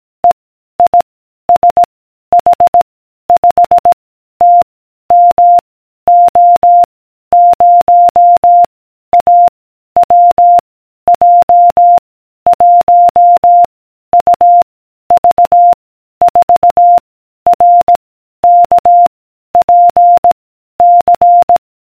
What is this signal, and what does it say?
Morse code, heard as a keyed tone, about 18 wpm, 700 Hz. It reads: E I S H 5 T M O 0 A W J 1 U V 4 R K P C